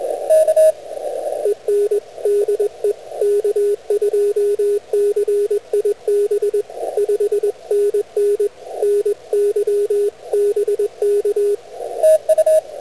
Morse code, heard as a keyed tone, about 21 wpm, 410 Hz, with radio noise. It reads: EN DE K2CIB 5NN NY BK